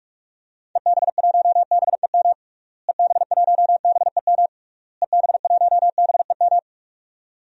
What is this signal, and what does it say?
Morse code, heard as a keyed tone, about 46 wpm, 700 Hz. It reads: EB1BEM EB1BEM EB1BEM
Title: DXCC-100 Call Signs - 6 Characters - Repeated 3x - 45wpm